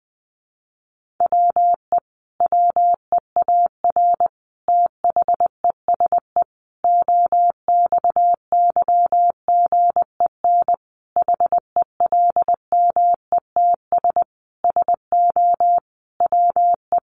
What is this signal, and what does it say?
Morse code, heard as a keyed tone, about 20 wpm, 700 Hz. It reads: WE WEAR THESE OXYGEN HELMETS SO WE
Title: A Sound Of Thunder 20wpm